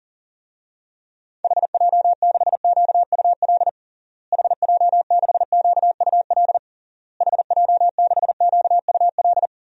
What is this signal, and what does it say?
Morse code, heard as a keyed tone, about 40 wpm, 700 Hz. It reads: HJ6QUL HJ6QUL HJ6QUL
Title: DXCC-100 Call Signs - 6 Characters - Repeated 3x - 40wpm